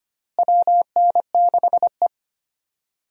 Morse code, heard as a keyed tone, about 25 wpm, 700 Hz. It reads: WN6E
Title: USA Call Signs - 4 Characters - Repeated 2x - 25wpm